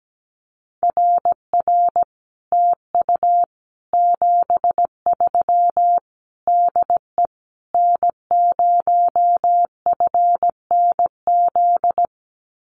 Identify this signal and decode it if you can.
Morse code, heard as a keyed tone, about 17 wpm, 700 Hz. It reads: RR TU 73 DE N0FNZ
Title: Common QSO Phrases Encore 17wpm